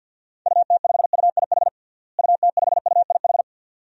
Morse code, heard as a keyed tone, about 47 wpm, 700 Hz. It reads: UT5UIH UT5UIH
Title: DXCC-100 Call Signs - 6 Characters - Repeated 2x - 50wpm